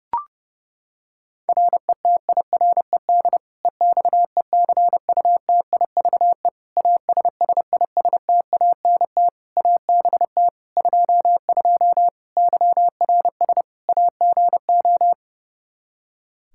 Morse code, heard as a keyed tone, about 31 wpm, 700 Hz. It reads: RETIRED EXECUTIVE ASSISTANT ABT 22 YRS AGO
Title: Practice QSO - Exchange 3 - More Info - 30wpm